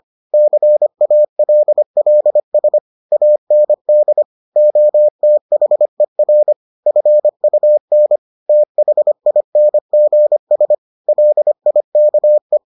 Morse code, heard as a keyed tone, about 25 wpm, 600 Hz. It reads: CALLS AND OTHER FUN THINGS LIKE